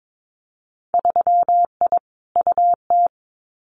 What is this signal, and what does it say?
Morse code, heard as a keyed tone, about 22 wpm, 700 Hz. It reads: 3I UT
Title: Winter Field Day Exchanges Encore 22wpm